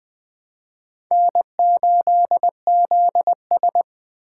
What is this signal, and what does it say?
Morse code, heard as a keyed tone, about 21 wpm, 700 Hz. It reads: N8ZS
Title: Speed Racing - USA Call Signs 17wpm